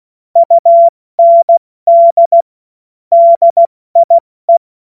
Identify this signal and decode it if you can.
Morse code, heard as a keyed tone, about 16 wpm, spaced as for 14 wpm, 680 Hz. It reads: UND DIE
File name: Morsekurs / Witze / Witz 20. September 2022 70 F80